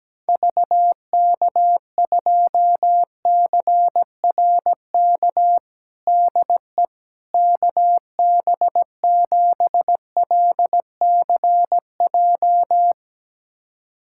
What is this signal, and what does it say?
Morse code, heard as a keyed tone, about 17 wpm, 700 Hz. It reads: VK2CRK DE KB7LCJ
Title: USA Callsign DE Callsign 17wpm